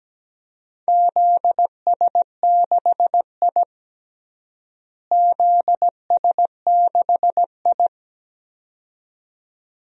Morse code, heard as a keyed tone, about 17 wpm, 700 Hz. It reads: ZS6I ZS6I
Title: DXCC-100 Call Signs - 4 Characters - Repeated 2x - 17wpm